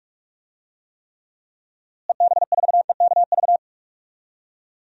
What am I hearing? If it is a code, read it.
Morse code, heard as a keyed tone, about 41 wpm, 700 Hz. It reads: EB4EKV
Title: DXCC-100 Call Signs - 6 Characters - Repeated 3x - 45wpm